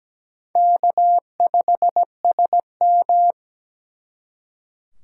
Morse code, heard as a keyed tone, about 20 wpm, 700 Hz. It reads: K5SM
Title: Speed Racing - USA 1x2 Call Signs 17wpm